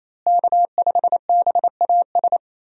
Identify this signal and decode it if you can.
Morse code, heard as a keyed tone, about 27 wpm, 700 Hz. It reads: K5BAS
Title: USA Call Signs - 5 Characters - Repeated 3x - 28wpm